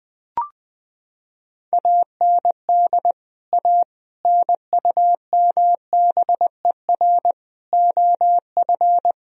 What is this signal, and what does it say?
Morse code, heard as a keyed tone, about 20 wpm, 700 Hz. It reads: AND A NUMBER OF
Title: Sentences from Top 1000 Words - Review Part 1 - 20wpm